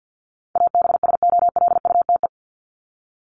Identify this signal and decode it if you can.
Morse code, heard as a keyed tone, about 48 wpm, 700 Hz. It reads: ABSOLUTE
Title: LY Suffix - Top 100 Words with Base Words 50wpm